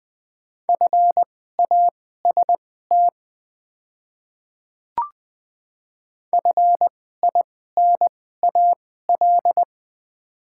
Morse code, heard as a keyed tone, about 20 wpm, spaced as for 15 wpm, 690 Hz. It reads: FAST FINAL
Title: 2x Character Spacing - Top 500 Words - 20wpm